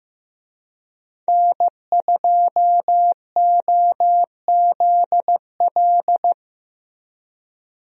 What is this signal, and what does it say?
Morse code, heard as a keyed tone, about 15 wpm, 700 Hz. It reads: N2OZL
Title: USA Call Signs - 5 Characters - Repeated 3x - 15wpm